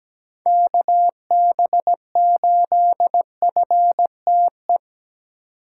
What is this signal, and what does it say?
Morse code, heard as a keyed tone, about 17 wpm, 700 Hz. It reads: KB8FTE
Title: USA Call Signs - 6 Characters - 8x Character Spacing - 17wpm